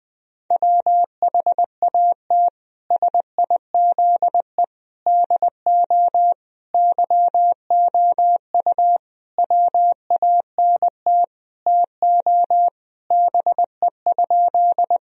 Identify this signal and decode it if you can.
Morse code, heard as a keyed tone, about 20 wpm, 700 Hz. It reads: WHAT SIZE DO YOU WANT TO BE?
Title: Sentences from Top 1000 Words - Review Part 2 - 20wpm